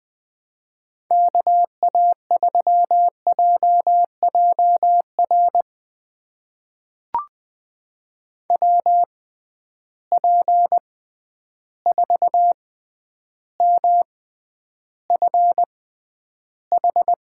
Morse code, heard as a keyed tone, about 20 wpm, spaced as for 10 wpm, 700 Hz. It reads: KA3JJR WP4MFH
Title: USA Call Signs - 6 Characters - 6x Character Spacing - 20wpm